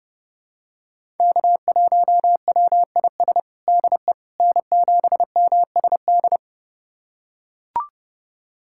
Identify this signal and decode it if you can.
Morse code, heard as a keyed tone, about 30 wpm, 700 Hz. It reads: K1WIS DE N7MSD
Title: USA Callsign DE Callsign 30wpm